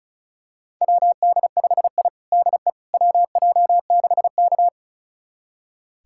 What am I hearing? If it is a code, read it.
Morse code, heard as a keyed tone, about 36 wpm, 700 Hz. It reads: WD5I DE WJ6K
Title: USA Callsign DE Callsign - 4 Characters - 35wpm